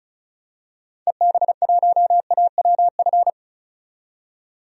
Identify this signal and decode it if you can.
Morse code, heard as a keyed tone, about 36 wpm, 700 Hz. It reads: EB1AWF
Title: DXCC-100 Call Signs - 6 Characters - 35wpm